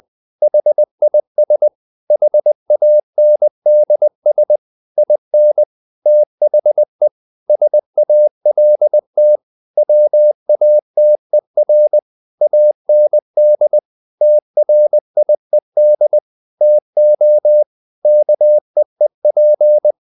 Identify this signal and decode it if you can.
Morse code, heard as a keyed tone, about 20 wpm, 600 Hz. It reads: HIS HANDS IN THE SALT WATER AND TRIED TO KEEP
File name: the old man and the sea part 2 20wpm